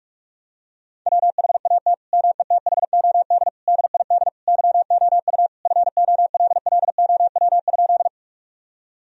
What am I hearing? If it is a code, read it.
Morse code, heard as a keyed tone, about 46 wpm, 700 Hz. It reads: WHAT METHOD DID YOU FOLLOW?
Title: Sentences from Top 1000 Words - Review Part 3 - 45wpm